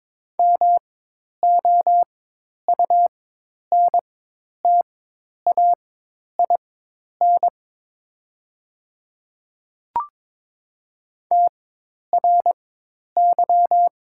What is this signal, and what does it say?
Morse code, heard as a keyed tone, about 22 wpm, spaced as for 10 wpm, 700 Hz. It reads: MOUNTAIN TRY
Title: 4x Character Spacing - Top 300 Words - 22wpm